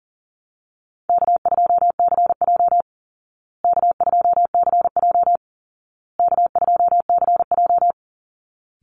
Morse code, heard as a keyed tone, about 40 wpm, 700 Hz. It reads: K2CJ K2CJ K2CJ
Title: Easy Call Signs Repeated 3x - 40wpm